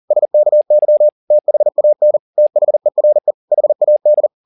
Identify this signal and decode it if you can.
Morse code, heard as a keyed tone, about 40 wpm, 600 Hz. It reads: SKY THAN THERE HAD
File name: the old man and the sea 2 40wpm